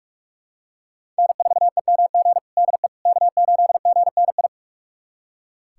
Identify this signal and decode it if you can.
Morse code, heard as a keyed tone, about 45 wpm, 700 Hz. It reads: N4EMG DE K8GNI
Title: USA Callsign DE Callsign - 5 Characters - 45wpm